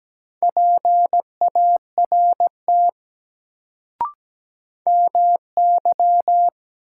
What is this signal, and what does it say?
Morse code, heard as a keyed tone, about 17 wpm, 700 Hz. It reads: PART MY
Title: Sets of 2 Words - New Word Drill - 17wpm